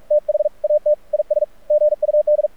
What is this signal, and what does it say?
Morse code, heard as a keyed tone, about 46 wpm, 600 Hz. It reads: THAT IS GUD
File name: archived / CRSnet